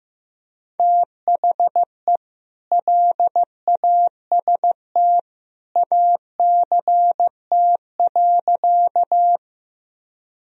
Morse code, heard as a keyed tone, about 15 wpm, 700 Hz. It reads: THE LAST ACT.
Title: Sentences from Top 1000 Words - Review Part 1 - 15wpm